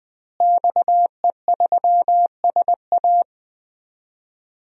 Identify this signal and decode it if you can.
Morse code, heard as a keyed tone, about 20 wpm, 700 Hz. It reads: XE3SA
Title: DXCC-100 Call Signs - 5 Characters - Repeated 2x - 20wpm